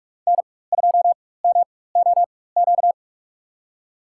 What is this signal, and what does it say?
Morse code, heard as a keyed tone, about 46 wpm, spaced as for 21 wpm, 700 Hz. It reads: N2MOQ
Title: USA Call Signs - 4x Character Spacing - 45wpm